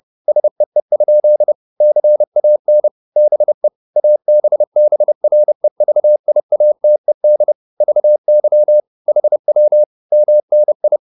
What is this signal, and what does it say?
Morse code, heard as a keyed tone, about 30 wpm, 600 Hz. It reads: SEE? CAN BE ABBREVIATED VY HW MNI